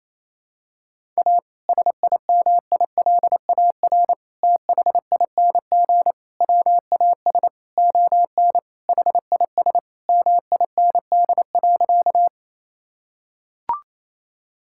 Morse code, heard as a keyed tone, about 28 wpm, 700 Hz. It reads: A SIMILAR THING WAS ON HIS MIND.
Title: Sentences from Top 1000 Words - Review Part 5 - 28wpm